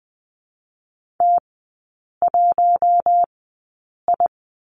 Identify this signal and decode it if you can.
Morse code, heard as a keyed tone, about 20 wpm, spaced as for 8 wpm, 700 Hz. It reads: T1I
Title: 041 Lesson 8e - Number 1 - Three Characters 20wpm